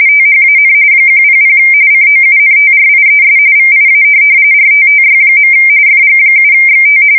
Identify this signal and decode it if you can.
RTTY, about 45.45 baud. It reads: THAN 100 COUNTRIES VIA RTTY. FG7XT
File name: fsk170 / teletype-RTTY